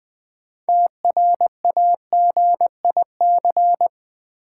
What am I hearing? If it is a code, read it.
Morse code, heard as a keyed tone, about 20 wpm, 700 Hz. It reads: TRAGIC
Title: 6 Letter Words - First 3 Letters Given - 20wpm